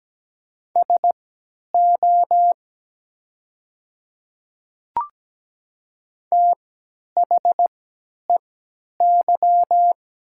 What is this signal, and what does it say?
Morse code, heard as a keyed tone, about 17 wpm, spaced as for 10 wpm, 700 Hz. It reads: SO THEY
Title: 3x Character Spacing - Top 100 Words - 17wpm